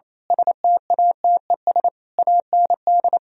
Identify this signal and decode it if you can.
Morse code, heard as a keyed tone, about 26 wpm, 700 Hz. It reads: STATES AND
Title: Speed-Racing - Sets of 5 Words 28wpm